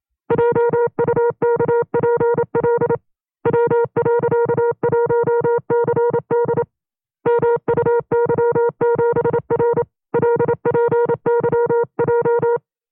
Morse code, heard as a keyed tone, about 27 wpm, 470 Hz. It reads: JUKPL W.1CD MUY7R LPYJ
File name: FreeFiles / CW-practice / 25WPM-side1